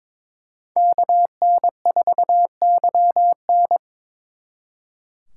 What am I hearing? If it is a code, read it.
Morse code, heard as a keyed tone, about 26 wpm, 700 Hz. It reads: KN4YN
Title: Speed Racing - USA Call Signs 22wpm